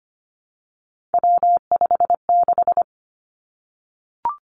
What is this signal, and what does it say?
Morse code, heard as a keyed tone, about 25 wpm, 700 Hz. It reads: W56
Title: Three Challenging Letter-Number Combinations 25wpm